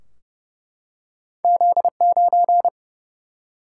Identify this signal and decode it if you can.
Morse code, heard as a keyed tone, about 30 wpm, 700 Hz. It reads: Z9
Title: Single-Letter Single-Number 30wpm